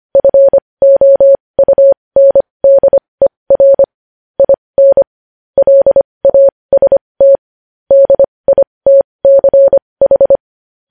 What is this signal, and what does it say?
Morse code, heard as a keyed tone, about 25 wpm, spaced as for 20 wpm, 550 Hz. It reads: FOUNDER IN LAST DITCH
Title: News Headlines at 20